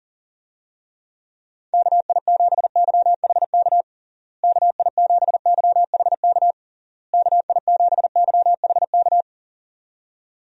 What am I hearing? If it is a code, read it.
Morse code, heard as a keyed tone, about 40 wpm, 700 Hz. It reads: KI7YHK KI7YHK KI7YHK
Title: USA Call Signs - 6 Characters - Repeated 3x - 40wpm